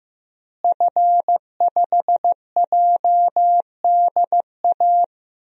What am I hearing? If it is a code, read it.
Morse code, heard as a keyed tone, about 15 wpm, 700 Hz. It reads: F5JDA
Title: DXCC-100 Call Signs - 5 Characters - Repeated 3x - 15wpm